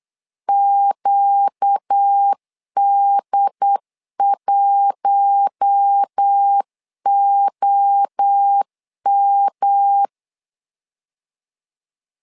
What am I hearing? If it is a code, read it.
Morse code, heard as a keyed tone, about 8 wpm, 800 Hz. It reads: QD1OM
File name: LEZIONI / 4 Q1 8wpm 3035